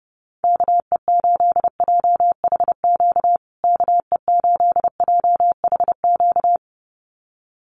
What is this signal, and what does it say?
Morse code, heard as a keyed tone, about 29 wpm, 700 Hz. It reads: KE8JHQ KE8JHQ
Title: USA Call Signs Repeated 2x 30wpm